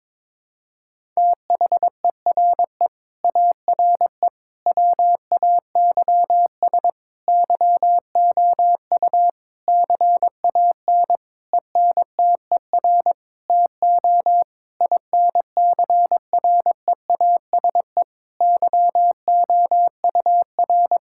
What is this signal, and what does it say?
Morse code, heard as a keyed tone, about 22 wpm, 700 Hz. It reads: THERE ARE WAYS YOU CAN ENTER TO INCREASE YOUR
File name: practice / 22wpm / Sentences with 8-Word Phrases - 22wpm